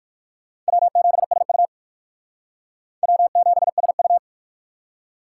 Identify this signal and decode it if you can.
Morse code, heard as a keyed tone, about 55 wpm, 700 Hz. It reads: W7SU W7SU
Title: Speed Racing - USA 1x2 Call Signs 45wpm